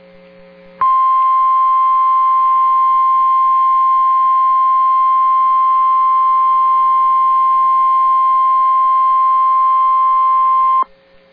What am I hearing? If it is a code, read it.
Emergency Alert System (SAME) tones (the 1050 Hz attention tone). It is heard from 1.0 to 10.8 s.